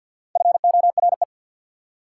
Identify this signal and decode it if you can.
Morse code, heard as a keyed tone, about 53 wpm, 690 Hz. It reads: FORE
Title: Prefixes and Suffixes 50wpm